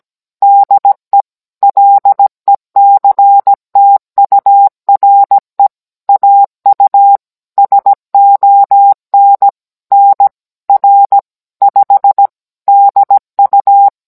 Morse code, heard as a keyed tone, about 17 wpm, 800 Hz. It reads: DE LECTURE AU SON NR 5 DU